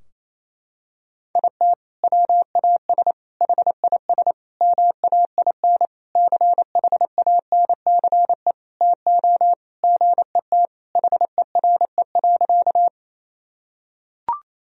Morse code, heard as a keyed tone, about 28 wpm, 700 Hz. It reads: IT WAS HIS MAIN CHANCE TO GET HERE.
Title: Sentences from Top 1000 Words - Review Part 2 - 28wpm